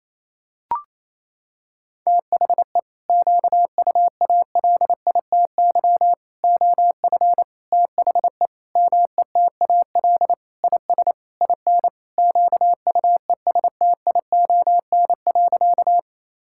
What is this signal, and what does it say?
Morse code, heard as a keyed tone, about 28 wpm, 700 Hz. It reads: THE QUALITY OF THE METAL IS IN QUESTION.
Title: Sentences from Top 1000 Words - Review Part 6 - 28wpm